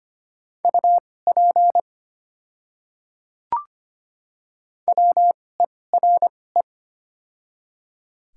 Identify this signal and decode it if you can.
Morse code, heard as a keyed tone, about 25 wpm, spaced as for 18 wpm, 690 Hz. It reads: UP WERE
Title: 2x Character Spacing - Top 200 Words - 25wpm